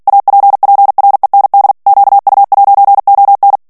Morse code, heard as a keyed tone, about 48 wpm, 800 Hz. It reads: APPREND QU'ON
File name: CW / FICHIERS-MP3 / FRANCAIS TEXTE CLAIR 2 VITESSE 45 MOTS MINUTES